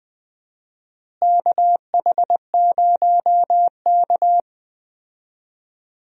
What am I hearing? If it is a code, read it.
Morse code, heard as a keyed tone, about 21 wpm, 700 Hz. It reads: KH0K
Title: USA Call Signs - 4 Characters - Repeated 2x - 20wpm